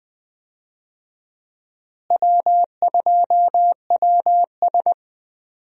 Morse code, heard as a keyed tone, about 20 wpm, 700 Hz. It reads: W2WS
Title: USA Call Signs - 4 Characters - Repeated 3x - 20wpm